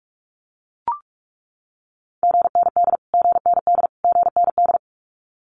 Morse code, heard as a keyed tone, about 44 wpm, 700 Hz. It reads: GND GND GND
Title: Abbreviations Repeated 3x - 45wpm